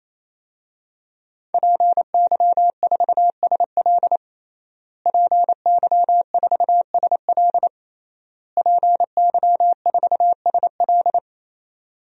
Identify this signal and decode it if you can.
Morse code, heard as a keyed tone, about 28 wpm, 700 Hz. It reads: PY4SL PY4SL PY4SL
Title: DXCC-100 Call Signs - 5 Characters - Repeated 3x - 28wpm